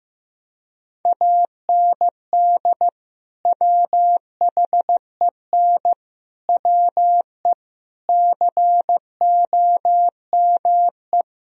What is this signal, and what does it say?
Morse code, heard as a keyed tone, about 15 wpm, 700 Hz. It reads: AND WHEN WE COME
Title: Speed-Racing - Sets of 5 Words 15wpm